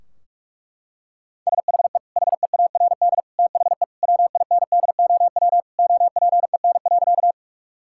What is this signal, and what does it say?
Morse code, heard as a keyed tone, about 46 wpm, 700 Hz. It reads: SHE HEARD THE WINDOW OPEN.
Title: Sentences from Top 1600 Words 45wpm